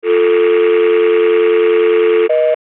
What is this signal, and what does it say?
Dial tone (North America), 350 + 440 Hz, with noise.
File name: Hollywood Sound Effects - Telephone - Dial Tone Sound Effect